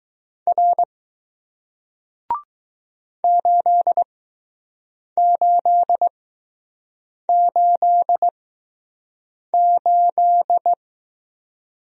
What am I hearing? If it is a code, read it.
Morse code, heard as a keyed tone, about 20 wpm, spaced as for 6 wpm, 700 Hz. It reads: R 8888